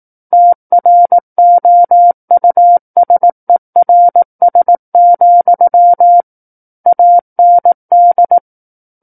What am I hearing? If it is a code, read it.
Morse code, heard as a keyed tone, about 18 wpm, 700 Hz. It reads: TROUSERS, AND
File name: 18 WPM